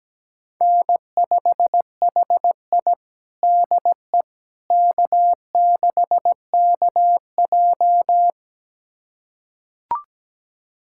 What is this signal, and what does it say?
Morse code, heard as a keyed tone, about 17 wpm, 700 Hz. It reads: N5HI DE K6KJ
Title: USA Callsign DE Callsign - 4 Characters - 17wpm